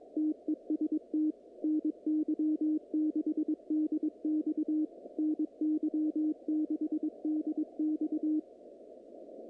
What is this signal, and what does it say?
Morse code, heard as a keyed tone, about 21 wpm, 310 Hz. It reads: TEST NY6DX NY6DX
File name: contests / naqp-cw-k5tr-op-k5pi- -2300